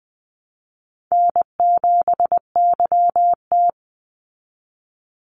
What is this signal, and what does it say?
Morse code, heard as a keyed tone, about 20 wpm, 700 Hz. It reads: N7YT
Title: Call Signs Repeated 3x - 20wpm